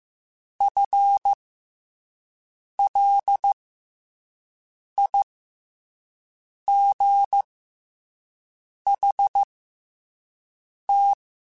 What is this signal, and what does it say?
Morse code, heard as a keyed tone, about 15 wpm, spaced as for 5 wpm, 800 Hz. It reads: FLIGHT
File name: FL Runs / FL Runs mp3s / FL Runs @5